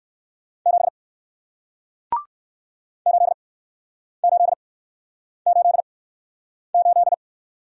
Morse code, heard as a keyed tone, about 61 wpm, 700 Hz. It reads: 7 8 8 8 8